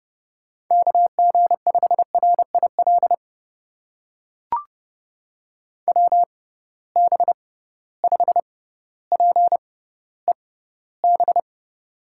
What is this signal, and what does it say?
Morse code, heard as a keyed tone, about 30 wpm, spaced as for 15 wpm, 700 Hz. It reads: KG5RIL WB5PEB
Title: USA Call Signs - 6 Characters - 6x Character Spacing - 30wpm